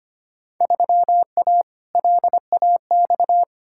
Morse code, heard as a keyed tone, about 25 wpm, 700 Hz. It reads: 3A LAX
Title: Field Day Exchanges - 25wpm